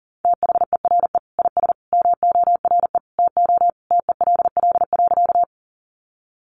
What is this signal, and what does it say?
Morse code, heard as a keyed tone, about 40 wpm, 700 Hz. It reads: THERE IS MORE TO TELL.
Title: Sentences from Top 200 Words 40wpm